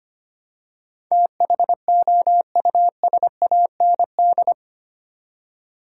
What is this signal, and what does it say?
Morse code, heard as a keyed tone, about 25 wpm, 690 Hz. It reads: THOUSAND
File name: 8 Letter Words - First 3 Letters Given - 25wpm